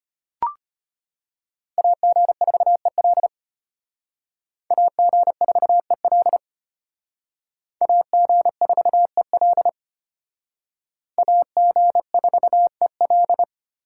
Morse code, heard as a keyed tone, about 31 wpm, 700 Hz. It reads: AG4EL AG4EL AG4EL AG4EL
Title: Speed Racing - USA 2x2 Call Signs 25wpm